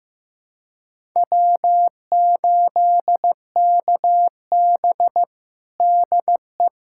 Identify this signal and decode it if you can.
Morse code, heard as a keyed tone, about 15 wpm, 700 Hz. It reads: W8KB DE
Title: USA Callsign DE Callsign - 4 Characters - 15wpm